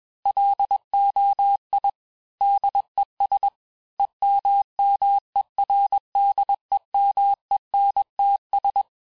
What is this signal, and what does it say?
Morse code, heard as a keyed tone, about 21 wpm, 800 Hz. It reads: LOI DES EMMERDEMENTS
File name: teste-FIN-de-QSO-vitesse-20mpm